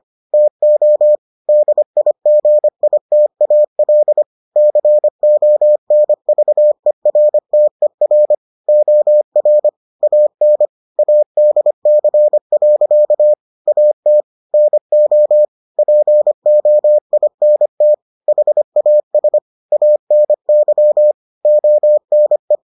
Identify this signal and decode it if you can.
Morse code, heard as a keyed tone, about 25 wpm, 600 Hz. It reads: TO DIGITAL CONVERTER OR AN ADC. AT NO POINT HAS ANY ONE